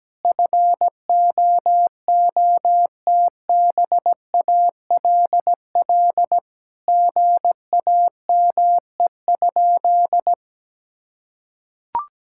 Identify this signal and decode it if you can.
Morse code, heard as a keyed tone, about 17 wpm, 700 Hz. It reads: FOOTBALL GAME?
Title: Sentences from Top 1600 Words 17wpm